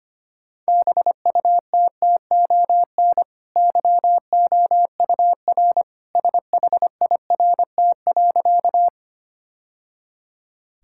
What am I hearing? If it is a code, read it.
Morse code, heard as a keyed tone, about 25 wpm, 700 Hz. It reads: BUTTON YOUR SHIRT.
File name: Easy Sentences with Hints - 25wpm